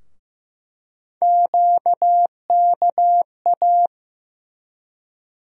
Morse code, heard as a keyed tone, about 15 wpm, 700 Hz. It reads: QKA
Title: USA Call Sign Postfixes - 3 Letters 15wpm